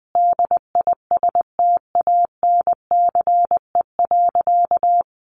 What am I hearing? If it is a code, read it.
Morse code, heard as a keyed tone, about 20 wpm, 700 Hz. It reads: DISTANCE.
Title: Sentences from Top 1200 Words 20wpm